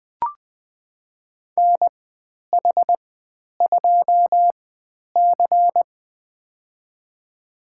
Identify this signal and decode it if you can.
Morse code, heard as a keyed tone, about 20 wpm, spaced as for 10 wpm, 700 Hz. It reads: NH2C
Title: 095 Lesson 16g - Letter c - Call Signs 20wpm